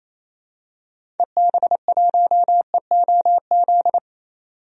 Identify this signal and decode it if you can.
Morse code, heard as a keyed tone, about 28 wpm, 700 Hz. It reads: EB1EOZ
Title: DXCC-100 Call Signs - 6 Characters - 28wpm